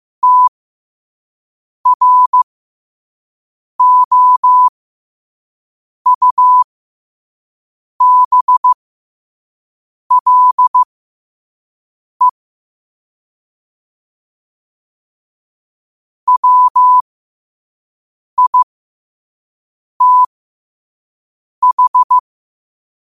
Morse code, heard as a keyed tone, about 15 wpm, spaced as for 5 wpm, 1000 Hz. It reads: TROUBLE WITH
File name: qotdfiles / 5wpm